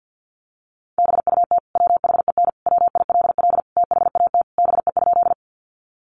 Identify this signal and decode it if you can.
Morse code, heard as a keyed tone, about 50 wpm, 700 Hz. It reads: BUT WHEN WILL THAT BE?
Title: Sentences from Top 100 Words 50wpm